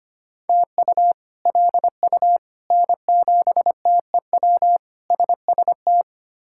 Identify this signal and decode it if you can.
Morse code, heard as a keyed tone, about 25 wpm, 700 Hz. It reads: TU LU N7TEW SST
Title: K1USN SST QSO Practice - 25wpm